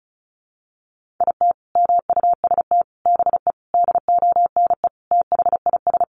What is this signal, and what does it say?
Morse code, heard as a keyed tone, about 34 wpm, 700 Hz. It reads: IT MUST BE DONE THIS
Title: Sentences from Top 1500 Words 35wpm